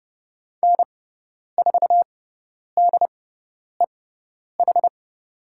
Morse code, heard as a keyed tone, about 30 wpm, spaced as for 10 wpm, 700 Hz. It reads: N4DEH
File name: Lesson 16g - Letter c - Call Signs 30wpm